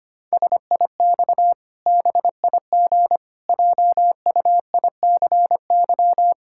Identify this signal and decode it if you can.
Morse code, heard as a keyed tone, about 25 wpm, 700 Hz. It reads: SIX BIG JUICY
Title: Pangram Challenge v2 - 25wpm